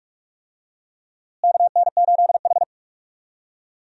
Morse code, heard as a keyed tone, about 46 wpm, 700 Hz. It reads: KN8H
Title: USA Call Signs - 4 Characters - Repeated 2x - 45wpm